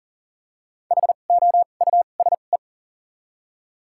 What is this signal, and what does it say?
Morse code, heard as a keyed tone, about 37 wpm, spaced as for 28 wpm, 700 Hz. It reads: HOUSE
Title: 2x Character Spacing - Top 200 Words - 40wpm